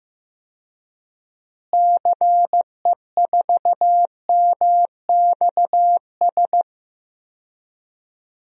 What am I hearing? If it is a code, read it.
Morse code, heard as a keyed tone, about 15 wpm, 700 Hz. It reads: CE4MXS
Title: DXCC-100 Call Signs - 6 Characters - Repeated 3x - 15wpm